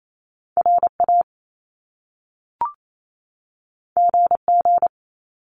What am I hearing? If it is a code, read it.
Morse code, heard as a keyed tone, about 28 wpm, 700 Hz. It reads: RA GG
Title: Top Two Letter Combinations in Words 28wpm